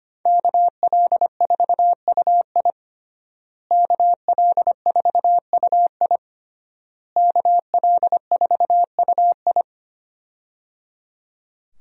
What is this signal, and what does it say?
Morse code, heard as a keyed tone, about 25 wpm, 700 Hz. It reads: KL4UI KL4UI KL4UI
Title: USA Call Signs - 5 Characters - Repeated 3x - 25wpm